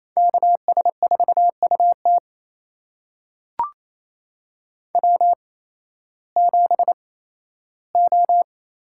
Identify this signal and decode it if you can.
Morse code, heard as a keyed tone, about 28 wpm, spaced as for 15 wpm, 700 Hz. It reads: KS4UT W7O
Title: USA Call Signs - 8x Character Spacing - 28wpm